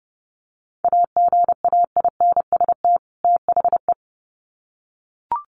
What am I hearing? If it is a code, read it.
Morse code, heard as a keyed tone, about 30 wpm, 700 Hz. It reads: AGAINST THE
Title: Sets of 2 Words 30wpm